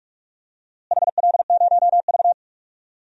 Morse code, heard as a keyed tone, about 46 wpm, 700 Hz. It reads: HL0V
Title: DXCC-100 Call Signs - 4 Characters - Repeated 3x - 45wpm